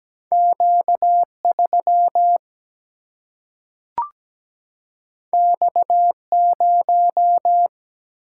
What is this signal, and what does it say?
Morse code, heard as a keyed tone, about 17 wpm, 700 Hz. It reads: Q3 X0
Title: Single-Letter Single-Number 17wpm